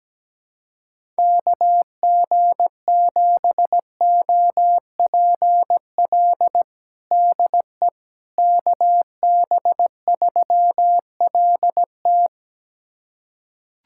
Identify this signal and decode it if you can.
Morse code, heard as a keyed tone, about 17 wpm, 700 Hz. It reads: KG7OPL DE KB3LT
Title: USA Callsign DE Callsign 17wpm